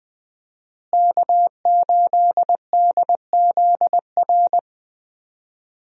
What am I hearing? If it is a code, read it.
Morse code, heard as a keyed tone, about 20 wpm, 700 Hz. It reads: K8DZR
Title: USA Call Signs - 5 Characters - Repeated 2x - 20wpm